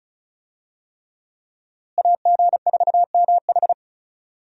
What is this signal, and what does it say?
Morse code, heard as a keyed tone, about 34 wpm, 700 Hz. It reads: AG4MH
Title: USA Call Signs - 5 Characters - Repeated 3x - 35wpm